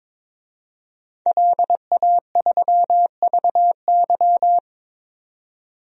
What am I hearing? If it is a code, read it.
Morse code, heard as a keyed tone, about 22 wpm, 700 Hz. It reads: LA3VY
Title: DXCC-100 Call Signs - 5 Characters - 22wpm